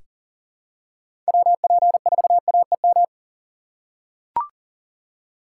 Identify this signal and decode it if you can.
Morse code, heard as a keyed tone, about 40 wpm, 700 Hz. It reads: WP4AEM
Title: USA Call Signs - 6 Characters - 4x Character Spacing - 40wpm